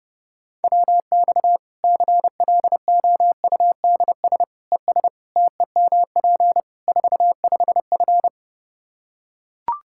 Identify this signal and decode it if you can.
Morse code, heard as a keyed tone, about 30 wpm, 700 Hz. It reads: WX CLOUDS ES TEMP 45F
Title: Practice QSO - Exchange 2 - Info - 30wpm